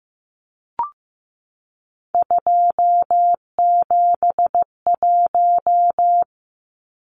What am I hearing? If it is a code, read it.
Morse code, heard as a keyed tone, about 15 wpm, 700 Hz. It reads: 271
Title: Numbers - 3-Digit 15wpm